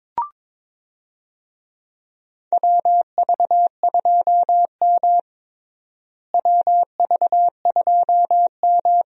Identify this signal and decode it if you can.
Morse code, heard as a keyed tone, about 22 wpm, 700 Hz. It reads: WV2M WV2M
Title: USA Call Signs - 4 Characters - Repeated 2x - 22wpm